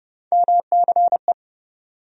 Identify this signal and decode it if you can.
Morse code, heard as a keyed tone, about 30 wpm, 700 Hz. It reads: MCE
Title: USA Call Sign Postfixes - 3 Letters 30wpm